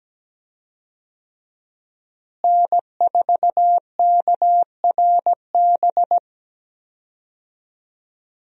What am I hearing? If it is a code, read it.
Morse code, heard as a keyed tone, about 17 wpm, 700 Hz. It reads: N4KRB
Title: USA Call Signs - 5 Characters - Repeated 2x - 17wpm